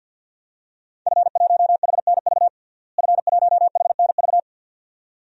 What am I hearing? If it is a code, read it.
Morse code, heard as a keyed tone, about 53 wpm, 700 Hz. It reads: F1HNV F1HNV
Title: DXCC-100 Call Signs - 5 Characters - Repeated 2x - 50wpm